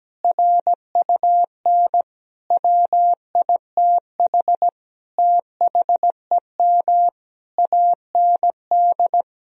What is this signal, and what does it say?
Morse code, heard as a keyed tone, about 17 wpm, 700 Hz. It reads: RUN WITH THEM AND
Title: Sentences from Top 1000 Words - Review Part 4 - 17wpm